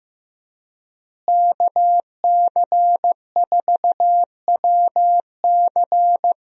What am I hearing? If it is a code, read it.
Morse code, heard as a keyed tone, about 15 wpm, 700 Hz. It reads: KC4WC
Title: USA Call Signs - 5 Characters - Repeated 3x - 15wpm